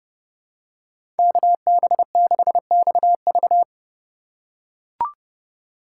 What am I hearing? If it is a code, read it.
Morse code, heard as a keyed tone, about 30 wpm, 700 Hz. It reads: KB6XV
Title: USA Call Signs - 2x Character Spacing - 30wpm